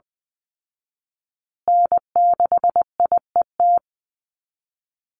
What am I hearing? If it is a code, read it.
Morse code, heard as a keyed tone, about 19 wpm, 700 Hz. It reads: N6IET
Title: Call Signs Repeated 2x - 20wpm